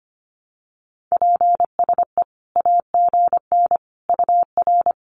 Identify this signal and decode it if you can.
Morse code, heard as a keyed tone, about 25 wpm, 700 Hz. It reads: PSE AGN UR
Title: Common QSO Phrases 25wpm